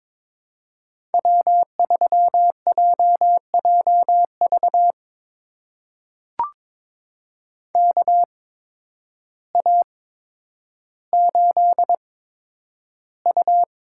Morse code, heard as a keyed tone, about 22 wpm, spaced as for 10 wpm, 700 Hz. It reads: W3JJV KA8U
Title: USA Call Signs - 8x Character Spacing - 22wpm